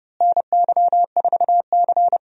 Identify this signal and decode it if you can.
Morse code, heard as a keyed tone, about 30 wpm, 700 Hz. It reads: NY4C
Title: USA Call Signs - 4 Characters - Repeated 3x - 30wpm